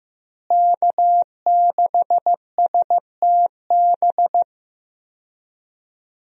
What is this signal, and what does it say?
Morse code, heard as a keyed tone, about 15 wpm, 700 Hz. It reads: K6STB
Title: USA Call Signs - 5 Characters - Repeated 3x - 15wpm